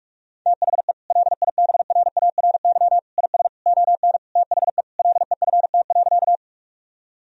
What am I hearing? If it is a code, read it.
Morse code, heard as a keyed tone, about 42 wpm, 700 Hz. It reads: THE LIBRARY IS ON THE LEFT.
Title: Sentences from Top 1000 Words - Review Part 4 - 45wpm